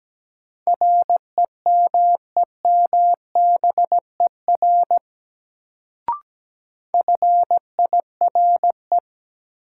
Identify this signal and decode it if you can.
Morse code, heard as a keyed tone, about 17 wpm, 700 Hz. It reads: REMEMBER FIRE
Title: Sentences from Top 400 Words - New Word Drill - 17wpm